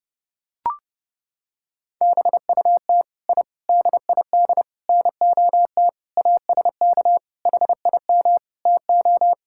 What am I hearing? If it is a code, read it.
Morse code, heard as a keyed tone, about 30 wpm, 700 Hz. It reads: BUT I DID NOT ASK HIM TO
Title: Sentences from Top 1000 Words - Review Part 5 - 30wpm